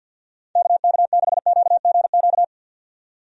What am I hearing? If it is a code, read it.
Morse code, heard as a keyed tone, about 51 wpm, 700 Hz. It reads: KK6QGQ
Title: USA Call Signs - 6 Characters - 50wpm